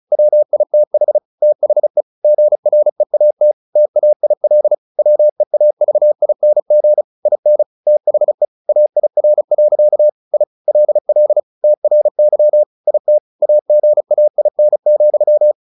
Morse code, heard as a keyed tone, about 34 wpm, 600 Hz. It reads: WITH THE GREAT TAIL WEAVING IN THE AIR. I LL TRY IT AGAIN,
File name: the old man and the sea 2 35wpm